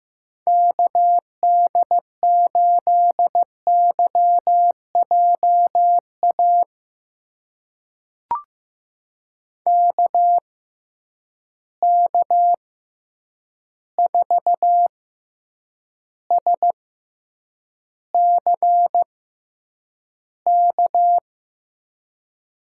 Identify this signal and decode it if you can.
Morse code, heard as a keyed tone, about 15 wpm, spaced as for 8 wpm, 700 Hz. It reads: KD8YJA KK4SCK
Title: USA Call Signs - 6 Characters - 6x Character Spacing - 15wpm